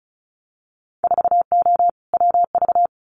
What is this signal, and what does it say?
Morse code, heard as a keyed tone, about 34 wpm, 700 Hz. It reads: 4O WV
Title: Winter Field Day Exchanges Encore 35wpm